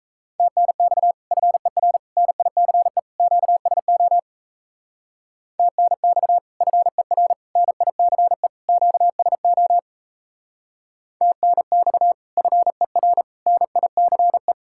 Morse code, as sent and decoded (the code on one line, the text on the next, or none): - -. -..- / ..-. .-. / -. .. -.-. / --.- ... --- / - -. -..- / ..-. .-. / -. .. -.-. / --.- ... --- / - -. -..- / ..-. .-. / -. .. -.-. .
TNX FER NICE QSO TNX FER NICE QSO TNX FER NICE